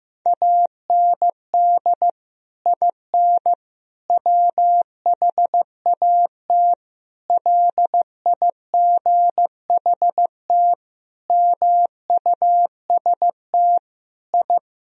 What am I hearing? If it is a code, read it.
Morse code, heard as a keyed tone, about 15 wpm, 700 Hz. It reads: AND IN WHAT LIGHT MUST I
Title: Sentences from Top 1000 Words - Review Part 2 - 15wpm